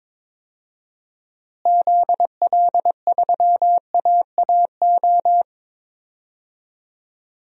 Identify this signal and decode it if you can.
Morse code, heard as a keyed tone, about 21 wpm, 700 Hz. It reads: ZL3AAO
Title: DXCC-100 Call Signs - 6 Characters - Repeated 2x - 22wpm